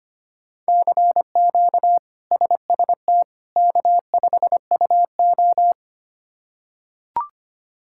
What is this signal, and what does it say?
Morse code, heard as a keyed tone, about 25 wpm, 700 Hz. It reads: CQ SST K5UO
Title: K1USN SST QSO Practice - 25wpm